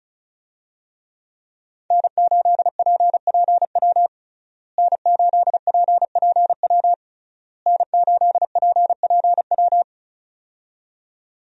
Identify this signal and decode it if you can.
Morse code, heard as a keyed tone, about 36 wpm, 700 Hz. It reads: N8PPW N8PPW N8PPW
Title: USA Call Signs - 5 Characters - Repeated 3x - 35wpm